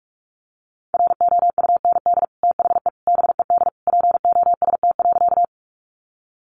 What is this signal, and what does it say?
Morse code, heard as a keyed tone, about 44 wpm, 700 Hz. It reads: ROUND THE BED POST.
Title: Sentences from Top 1000 Words 45wpm